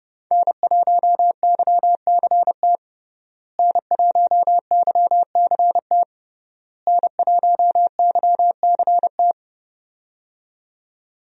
Morse code, heard as a keyed tone, about 32 wpm, 700 Hz. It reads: N1YCT N1YCT N1YCT
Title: USA Call Signs - 5 Characters - Repeated 3x - 30wpm